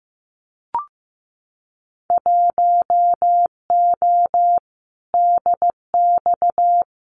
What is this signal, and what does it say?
Morse code, heard as a keyed tone, about 15 wpm, 700 Hz. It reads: 1O DX
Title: Winter Field Day Exchanges Encore 15wpm